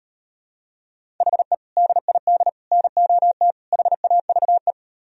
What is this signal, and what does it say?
Morse code, heard as a keyed tone, about 38 wpm, 700 Hz. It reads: HE DID NOT HAVE